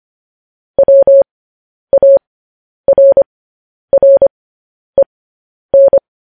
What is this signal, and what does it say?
Morse code, heard as a keyed tone, about 25 wpm, spaced as for 10 wpm, 550 Hz. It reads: WARREN
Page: news-headlines-in-morse-code-at-10-wpm